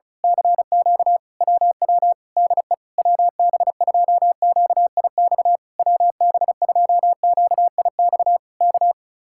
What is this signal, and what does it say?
Morse code, heard as a keyed tone, about 32 wpm, 700 Hz. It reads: CQ WW DE WB2QIX WB2QIX K
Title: Speed-Racing Common QSO Phrases Encore - 35wpm